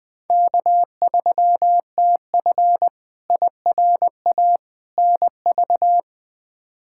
Morse code, heard as a keyed tone, about 21 wpm, 700 Hz. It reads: K3TF IRA NV
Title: K1USN SST QSO Practice - 20wpm